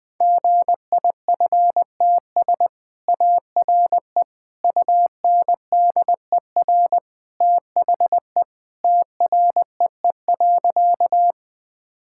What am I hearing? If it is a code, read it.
Morse code, heard as a keyed tone, about 20 wpm, 700 Hz. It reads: GIFTS ARE UNDER THE TREE.
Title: Easy Sentences with Hints - 20wpm